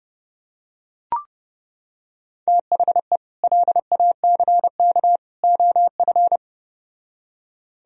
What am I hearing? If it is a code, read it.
Morse code, heard as a keyed tone, about 29 wpm, 700 Hz. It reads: THE LACK OF